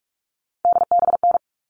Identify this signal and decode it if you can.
Morse code, heard as a keyed tone, about 44 wpm, 700 Hz. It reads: DBN
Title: All Cut Numbers - 3-Digit 45wpm